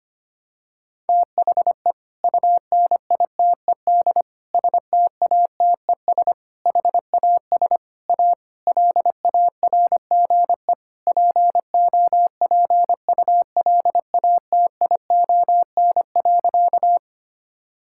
Morse code, heard as a keyed tone, about 25 wpm, 700 Hz. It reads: THE UNITED STATES HAS A LARGE POPULATION.
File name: practice / 25wpm / Sentences with 4-Word Phrases - 25wpm